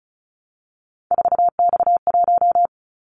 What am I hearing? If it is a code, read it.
Morse code, heard as a keyed tone, about 36 wpm, 700 Hz. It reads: 4X1
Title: Callsign Prefixes for Top 50 DXCC - 35wpm